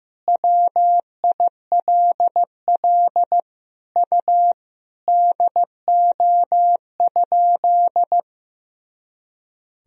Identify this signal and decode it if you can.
Morse code, heard as a keyed tone, about 15 wpm, 700 Hz. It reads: WILL U DO?
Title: Sentences from Top 100 Words and Abbreviations 15wpm